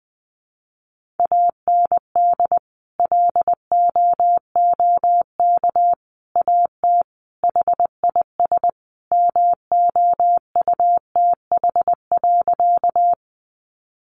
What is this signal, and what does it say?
Morse code, heard as a keyed tone, about 20 wpm, 700 Hz. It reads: AND LOOK AT HIS MOUTH.
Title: Sentences from Top 1500 Words 20wpm